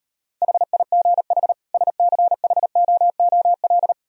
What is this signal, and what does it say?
Morse code, heard as a keyed tone, about 32 wpm, 690 Hz. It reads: HIGH SCHOOL
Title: Speed-Racing - Sets of 2 Words 28wpm